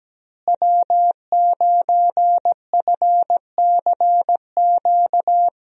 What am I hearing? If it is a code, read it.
Morse code, heard as a keyed tone, about 17 wpm, 700 Hz. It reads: W9FCQ
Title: USA Call Signs - 5 Characters - Repeated 3x - 17wpm